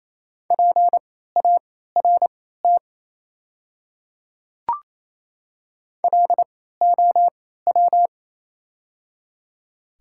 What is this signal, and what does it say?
Morse code, heard as a keyed tone, about 28 wpm, spaced as for 16 wpm, 700 Hz. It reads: PART LOW
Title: 3x Character Spacing - Top 300 Words - 28wpm